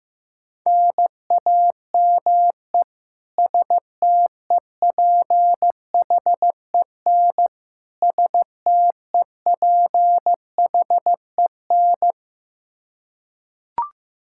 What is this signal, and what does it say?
Morse code, heard as a keyed tone, about 15 wpm, 700 Hz. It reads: NAME STEPHEN STEPHEN
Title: Practice QSO - Exchange 1 - Beginning - 15wpm